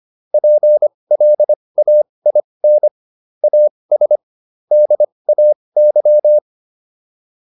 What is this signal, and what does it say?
Morse code, heard as a keyed tone, about 25 wpm, spaced as for 20 wpm, 600 Hz. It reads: PLAIN AS DAY